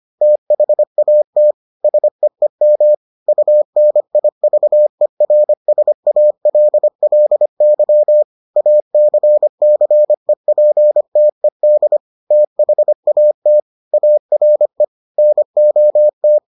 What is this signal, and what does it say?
Morse code, heard as a keyed tone, about 25 wpm, 600 Hz. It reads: THAT SEEM UNIVERSALLY ACCEPTED THAT ARE NOT